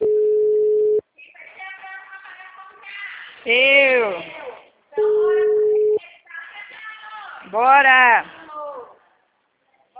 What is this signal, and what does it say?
Ringback tone (Europe, 425 Hz), with speech.